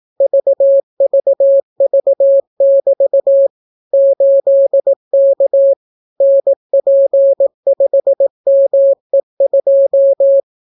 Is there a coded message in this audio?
Yes: Morse code from 0.2 to 10.4 s, heard as a keyed tone, about 18 wpm, 550 Hz. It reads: VVV<BT> 8K NP5ME2